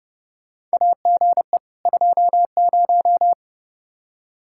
Morse code, heard as a keyed tone, about 30 wpm, 700 Hz. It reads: AGE 20
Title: Practice QSO - Exchange 3 - More Info - 30wpm